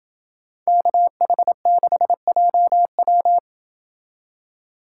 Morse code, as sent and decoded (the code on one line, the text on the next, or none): -.- .... -.... .--- .--
KH6JW